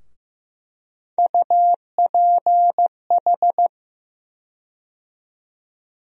Morse code, heard as a keyed tone, about 15 wpm, 700 Hz. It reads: UPH
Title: USA Call Sign Postfixes - 3 Letters 15wpm